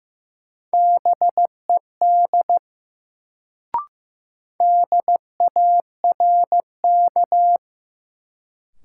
Morse code, heard as a keyed tone, about 15 wpm, 700 Hz. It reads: BED DARK
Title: Sentences from Top 500 Words - New Word Drill - 15wpm